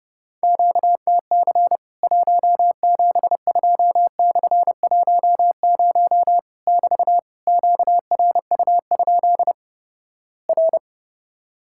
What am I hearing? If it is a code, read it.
Morse code, heard as a keyed tone, about 32 wpm, 700 Hz. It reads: QTC 172/10 <BT> QRU? R
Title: Worked All Europe DX Contest - QTC Exchange - 30wpm